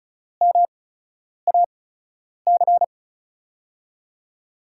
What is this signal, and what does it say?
Morse code, heard as a keyed tone, about 36 wpm, spaced as for 10 wpm, 700 Hz. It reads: MAC
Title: USA Call Signs - 6 Characters - 8x Character Spacing - 35wpm